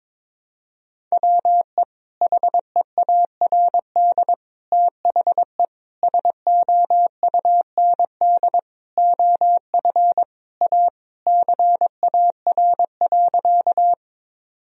Morse code, heard as a keyed tone, about 22 wpm, 700 Hz. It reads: WE HEARD THE SOUND OF A CAR.
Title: Sentences from Top 1000 Words - Review Part 1 - 22wpm